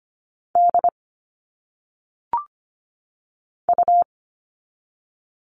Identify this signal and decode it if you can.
Morse code, heard as a keyed tone, about 25 wpm, 700 Hz. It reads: D U
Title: All Cut Numbers - 1-Digit 25wpm